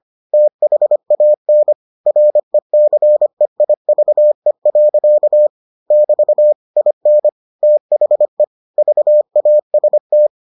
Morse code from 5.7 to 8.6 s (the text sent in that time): <BT> IN THE